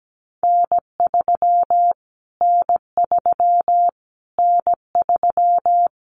Morse code, heard as a keyed tone, about 17 wpm, 700 Hz. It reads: N3 N3 N3
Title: Callsign Prefixes for Top 25 DXCC - Repeated 3x - 17wpm